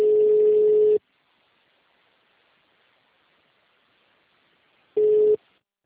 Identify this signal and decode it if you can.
Ringback tone (Europe, 425 Hz).
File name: out- -2015- - - .53